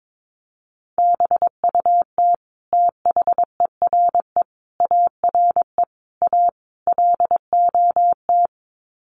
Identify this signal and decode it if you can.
Morse code, heard as a keyed tone, about 22 wpm, 700 Hz. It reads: BUT THERE ARE A LOT
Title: Sets of 5 Words Encore2 22wpm